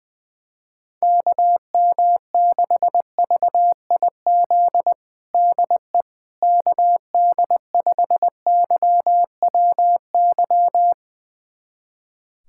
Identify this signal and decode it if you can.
Morse code, heard as a keyed tone, about 21 wpm, 700 Hz. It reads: KM6VIZ DE KD5YWY
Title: USA Callsign DE Callsign - 6 Characters - 20wpm